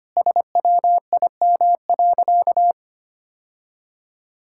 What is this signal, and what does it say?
Morse code, heard as a keyed tone, about 25 wpm, 700 Hz. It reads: SWIM.
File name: Easy Sentences with Hints - 25wpm